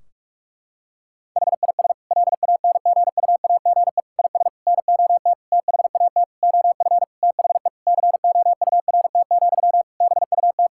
Morse code, heard as a keyed tone, about 41 wpm, 700 Hz. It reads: HIS LANGUAGE IS NOT THAT OF THE COURT, BUT
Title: Sentences from Top 1600 Words 45wpm